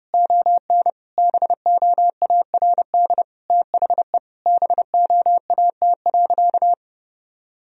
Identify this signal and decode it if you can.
Morse code, heard as a keyed tone, about 30 wpm, 700 Hz. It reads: ON BOARD THE BOAT.
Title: Sentences from Top 1000 Words - Review Part 4 - 30wpm